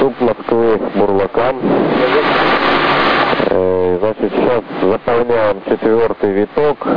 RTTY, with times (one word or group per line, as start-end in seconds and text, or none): none